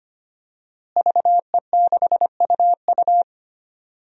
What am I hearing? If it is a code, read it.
Morse code, heard as a keyed tone, about 25 wpm, 700 Hz. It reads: VE6UU
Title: DXCC-100 Call Signs - 5 Characters - Repeated 2x - 25wpm